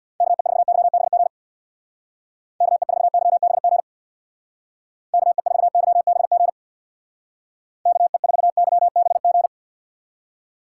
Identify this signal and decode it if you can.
Morse code, heard as a keyed tone, about 65 wpm, 700 Hz. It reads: KE4YBG KE4YBG KE4YBG KE4YBG